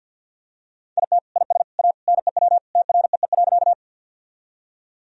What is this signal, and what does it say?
Morse code, heard as a keyed tone, about 50 wpm, 700 Hz. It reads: IT IS A NEW TREE.
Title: Sentences from Top 1000 Words - Review Part 1 - 50wpm